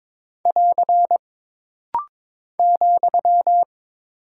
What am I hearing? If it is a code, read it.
Morse code, heard as a keyed tone, about 22 wpm, 700 Hz. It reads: <AR> ,
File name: practice / 22wpm / Single Punctuation-Prosign - Rapid-Fire - 22wpm